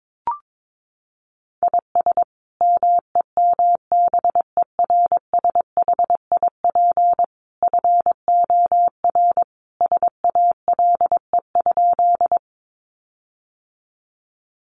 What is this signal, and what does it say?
Morse code, heard as a keyed tone, about 22 wpm, 700 Hz. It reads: IS MEMBERSHIP FOR SALE?
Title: Sentences from Top 1400 Words 22wpm